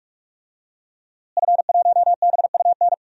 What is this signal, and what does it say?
Morse code, heard as a keyed tone, about 46 wpm, 700 Hz. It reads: F1BUN
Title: DXCC-100 Call Signs - 5 Characters - Repeated 3x - 45wpm